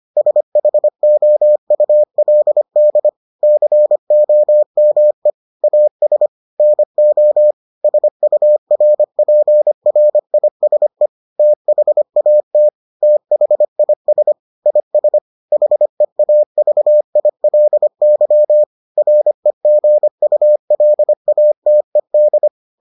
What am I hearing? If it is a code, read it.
Morse code, heard as a keyed tone, about 25 wpm, 600 Hz. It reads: SHOULD COME AS NO SURPRISE THAT THIS IS HEAVILY REGULATED